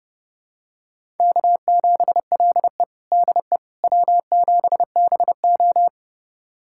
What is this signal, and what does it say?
Morse code, heard as a keyed tone, about 30 wpm, 700 Hz. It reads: K7LE DE W7BO
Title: USA Callsign DE Callsign - 4 Characters - 30wpm